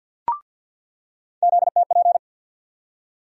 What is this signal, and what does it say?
Morse code, heard as a keyed tone, about 49 wpm, 700 Hz. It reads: ZTP
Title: USA Call Sign Postfixes - 3 Letters 50wpm